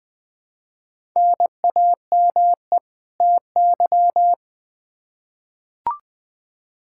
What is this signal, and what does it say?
Morse code, heard as a keyed tone, about 20 wpm, 700 Hz. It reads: NAME TY
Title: Practice QSO - Exchange 1 - Beginning - 20wpm